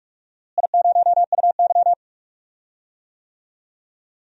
Morse code, heard as a keyed tone, about 46 wpm, 700 Hz. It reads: I0UY
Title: DXCC-100 Call Signs - 4 Characters - Repeated 3x - 45wpm